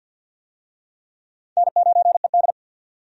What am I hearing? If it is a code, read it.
Morse code, heard as a keyed tone, about 53 wpm, 700 Hz. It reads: N9ED
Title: USA Call Signs - 4 Characters - 50wpm